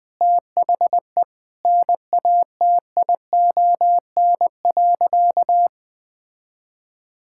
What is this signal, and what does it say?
Morse code, heard as a keyed tone, about 20 wpm, 700 Hz. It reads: THE NATION.
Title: Sentences with 7-Word Phrases - Spoken Beginning - 20wpm